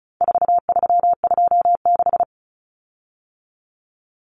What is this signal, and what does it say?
Morse code, heard as a keyed tone, about 34 wpm, 700 Hz. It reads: 4326
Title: Common Cut Numbers - 4-Digit 35wpm